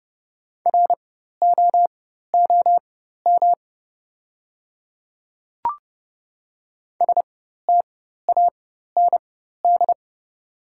Morse code, heard as a keyed tone, about 32 wpm, spaced as for 14 wpm, 690 Hz. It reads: ROOM STAND
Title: 4x Character Spacing - Top 500 Words - 30wpm